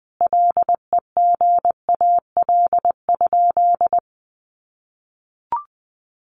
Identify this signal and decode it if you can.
Morse code, heard as a keyed tone, about 20 wpm, 700 Hz. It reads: LEGAL?
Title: Sentences from Top 1200 Words 20wpm